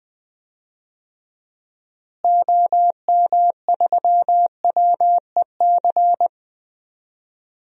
Morse code, heard as a keyed tone, about 21 wpm, 690 Hz. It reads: OM3WEC
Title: DXCC-100 Call Signs - 6 Characters - Repeated 2x - 20wpm